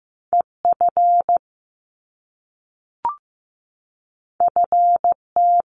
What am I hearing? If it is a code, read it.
Morse code, heard as a keyed tone, about 15 wpm, 700 Hz. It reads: EF FT
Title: Top Two Letter Combinations in Words 15wpm